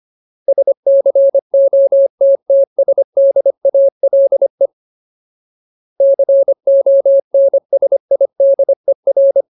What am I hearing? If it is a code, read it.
Morse code, heard as a keyed tone, about 25 wpm, 550 Hz. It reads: SCOTTSDALE CONSIDER
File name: Ultimate Word List / 25WPM Ultimate